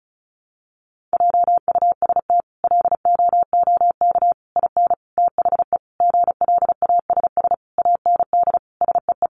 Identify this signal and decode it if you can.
Morse code, heard as a keyed tone, about 34 wpm, 700 Hz. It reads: JUST LOOK IN THE GLASS AND SEE
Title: Sentences from Top 1300 Words 35wpm